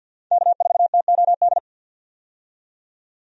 Morse code, heard as a keyed tone, about 52 wpm, 700 Hz. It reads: K4TOD
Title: USA Call Signs - 5 Characters - 50wpm